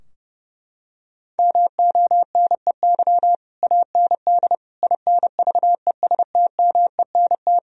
Morse code, heard as a keyed tone, about 30 wpm, 700 Hz. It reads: MONEY AND INVESTMENT
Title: Sentences from Top 1000 Words - Review Part 4 - 30wpm